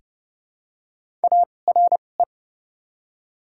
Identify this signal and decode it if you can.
Morse code, heard as a keyed tone, about 30 wpm, spaced as for 22 wpm, 700 Hz. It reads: ARE
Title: 2x Character Spacing - Top 200 Words - 30wpm